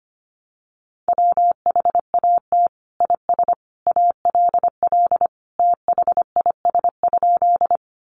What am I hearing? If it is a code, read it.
Morse code, heard as a keyed tone, about 25 wpm, 700 Hz. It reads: WHAT IS ALL THIS?
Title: Sentences from Top 100 Words 25wpm